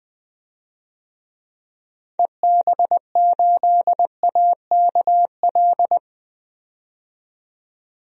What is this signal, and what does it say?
Morse code, heard as a keyed tone, about 20 wpm, 700 Hz. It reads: EB8AKL
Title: DXCC-100 Call Signs - 6 Characters - 20wpm